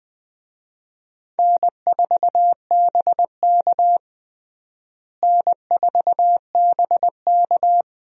Morse code, heard as a keyed tone, about 20 wpm, 700 Hz. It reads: N4BK N4BK
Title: USA Call Signs - 4 Characters - Repeated 2x - 20wpm